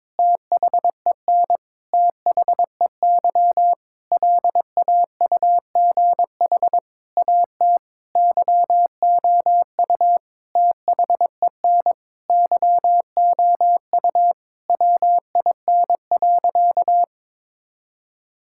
Morse code, heard as a keyed tone, about 22 wpm, 700 Hz. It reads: THEN THEY LAUGH AT YOU THEN YOU WIN.
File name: practice / 22wpm / Sentences with 8-Word Phrases - 22wpm